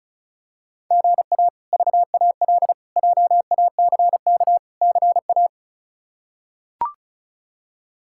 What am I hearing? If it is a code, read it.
Morse code, heard as a keyed tone, about 36 wpm, 700 Hz. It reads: GA VAL JACK CA
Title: K1USN SST QSO Practice - 35wpm